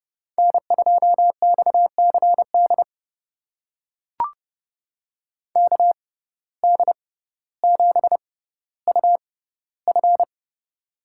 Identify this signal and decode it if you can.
Morse code, heard as a keyed tone, about 32 wpm, spaced as for 16 wpm, 690 Hz. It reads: N2XCD KD7UF
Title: USA Call Signs - 5 Characters - 6x Character Spacing - 30wpm